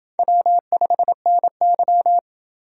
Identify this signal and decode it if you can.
Morse code, heard as a keyed tone, about 29 wpm, 700 Hz. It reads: W5NY